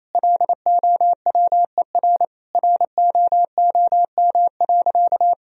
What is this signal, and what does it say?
Morse code, heard as a keyed tone, about 28 wpm, 700 Hz. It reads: LOWER ROOM.
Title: Sentences from Top 1000 Words - Review Part 4 - 28wpm